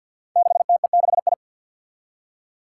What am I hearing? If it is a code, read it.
Morse code, heard as a keyed tone, about 47 wpm, 700 Hz. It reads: 6TE6I
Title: Five Challenging Letter-Number Combinations 50wpm